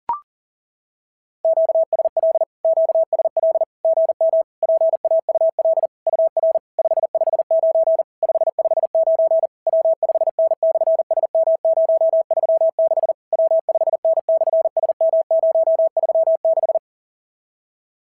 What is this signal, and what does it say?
Morse code, heard as a keyed tone, about 41 wpm, 650 Hz. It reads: QSL QSL GM PAUL UR 559 559 W5N/SM036 W5N/SM036
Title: SOTA QSO Practice 40wpm